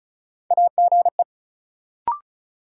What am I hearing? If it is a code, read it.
Morse code, heard as a keyed tone, about 36 wpm, 700 Hz. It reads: AGE
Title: Sets of 3 Words - New Word Drill - 35wpm